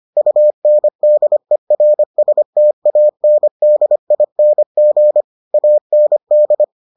Morse code, heard as a keyed tone, about 25 wpm, 600 Hz. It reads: UNDERSTANDING AND